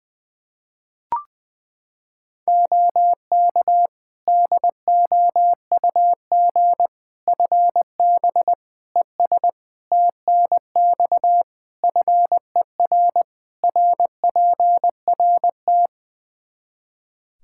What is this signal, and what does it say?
Morse code, heard as a keyed tone, about 20 wpm, 700 Hz. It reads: OK DOUG FB ES TNX FER RPRT
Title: Practice QSO - Exchange 2 - Info - 20wpm